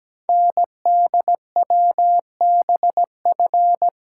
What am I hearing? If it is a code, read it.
Morse code, heard as a keyed tone, about 17 wpm, 700 Hz. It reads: NDWBF
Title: Code Groups - 5 Letters - 17wpm